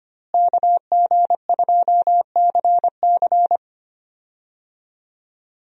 Morse code, heard as a keyed tone, about 25 wpm, 700 Hz. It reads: KG2CC
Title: USA Call Signs - 5 Characters - Repeated 2x - 25wpm